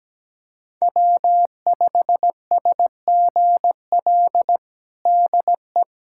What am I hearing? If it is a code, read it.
Morse code, heard as a keyed tone, about 17 wpm, 700 Hz. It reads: W5SGL DE
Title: USA Callsign DE Callsign 17wpm